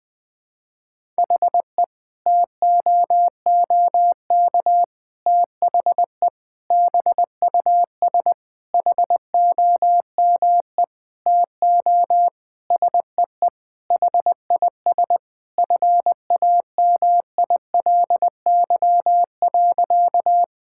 Morse code, heard as a keyed tone, about 21 wpm, 700 Hz. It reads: HE TOOK THE BUS HOME TO SEE HIS FAMILY.
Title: Sentences from Top 1000 Words - Review Part 6 - 20wpm